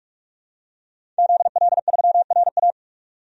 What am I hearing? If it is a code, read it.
Morse code, heard as a keyed tone, about 46 wpm, 700 Hz. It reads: ZL3RA
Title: DXCC-100 Call Signs - 5 Characters - 45wpm